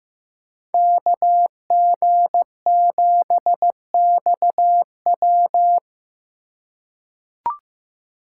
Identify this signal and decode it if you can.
Morse code, heard as a keyed tone, about 15 wpm, 700 Hz. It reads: KG7XW
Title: USA Call Signs - 5 Characters - 3x Character Spacing - 15wpm